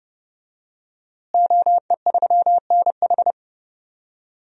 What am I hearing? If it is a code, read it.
Morse code, heard as a keyed tone, about 32 wpm, 700 Hz. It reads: OE3NH
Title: DXCC-100 Call Signs - 5 Characters - Repeated 2x - 30wpm